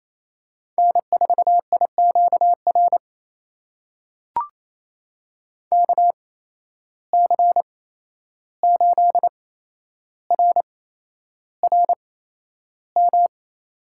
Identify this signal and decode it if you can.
Morse code, heard as a keyed tone, about 28 wpm, spaced as for 11 wpm, 700 Hz. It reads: N4IQR KC8RRM
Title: USA Call Signs - 8x Character Spacing - 28wpm